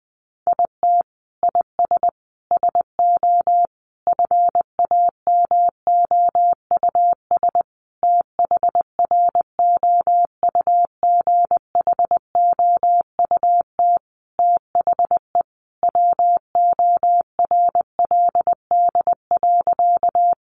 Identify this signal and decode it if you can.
Morse code, heard as a keyed tone, about 20 wpm, 700 Hz. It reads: IT IS SO FAMOUS THROUGHOUT THE WORLD.
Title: Sentences from Top 1500 Words 20wpm